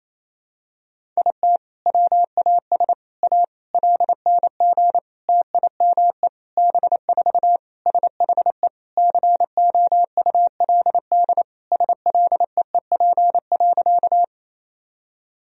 Morse code, heard as a keyed tone, about 28 wpm, 700 Hz. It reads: IT WAS A LNG TIME B4 SHE COULD SLEEP.
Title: Sentences from Top 500 Words and Abbreviations 28wpm